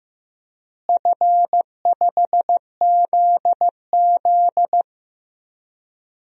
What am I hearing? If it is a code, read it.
Morse code, heard as a keyed tone, about 16 wpm, 700 Hz. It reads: F5ZZ
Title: DXCC-100 Call Signs - 4 Characters - Repeated 3x - 15wpm